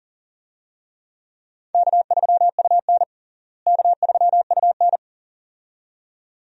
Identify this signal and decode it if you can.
Morse code, heard as a keyed tone, about 40 wpm, 700 Hz. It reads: K3UN K3UN
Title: USA Call Signs - 4 Characters - Repeated 2x - 40wpm